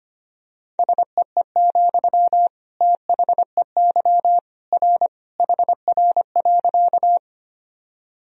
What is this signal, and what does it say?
Morse code, heard as a keyed tone, about 25 wpm, 700 Hz. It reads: SEE, THEY R HR.
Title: Sentences from Top 200 Words and Abbreviations 25wpm